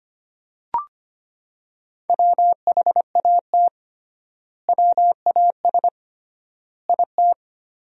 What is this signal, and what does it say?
Morse code, heard as a keyed tone, about 25 wpm, 700 Hz. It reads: WHAT WAS IT
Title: 3x Word-Spacing - Sentences from Top 100 Words - 25wpm